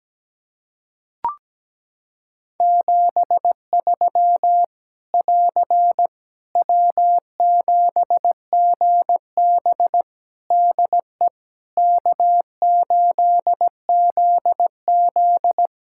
Morse code, heard as a keyed tone, about 17 wpm, 700 Hz. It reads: 73 <AR> W7GB DE K8ZZ
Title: Practice QSO - Exchange 4 - Ending - 17wpm